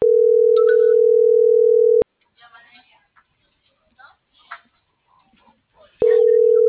Ringback tone (North America), 440 + 480 Hz.